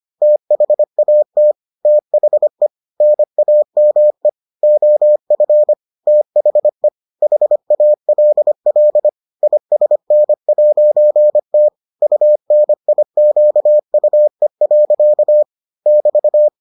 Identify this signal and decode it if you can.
Morse code, heard as a keyed tone, about 25 wpm, 600 Hz. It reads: THAT THE NAME OF THE HALL ISN'T UNIQUE. <BT>